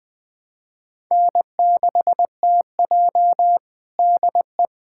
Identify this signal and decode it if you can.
Morse code, heard as a keyed tone, about 20 wpm, 700 Hz. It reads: N6TJ DE
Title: USA Callsign DE Callsign - 4 Characters - 20wpm